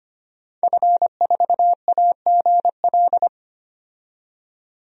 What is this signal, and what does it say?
Morse code, heard as a keyed tone, about 25 wpm, 700 Hz. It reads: F4AGL
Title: DXCC-100 Call Signs - 5 Characters - 25wpm